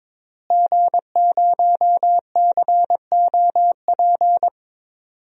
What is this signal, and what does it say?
Morse code, heard as a keyed tone, about 22 wpm, 700 Hz. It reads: G0COP
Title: DXCC-100 Call Signs - 5 Characters - Repeated 3x - 22wpm